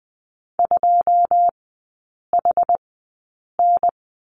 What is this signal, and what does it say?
Morse code, heard as a keyed tone, about 20 wpm, spaced as for 8 wpm, 700 Hz. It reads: 2HN
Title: 159 Lesson 26e - Letter g - Three Characters 20wpm